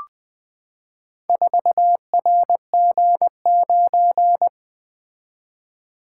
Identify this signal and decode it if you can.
Morse code, heard as a keyed tone, about 20 wpm, 700 Hz. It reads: 4RG9
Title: Four Letter-Number Combinations - 20wpm